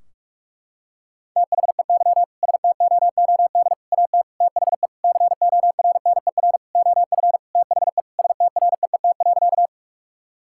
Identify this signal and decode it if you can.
Morse code, heard as a keyed tone, about 46 wpm, 700 Hz. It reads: THEY STOOD AT THE CORNER OF THE STREET.
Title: Sentences from Top 1600 Words 45wpm